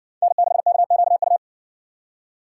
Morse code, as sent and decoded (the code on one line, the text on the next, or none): -. -.... -.- --.- ..-
N6KQU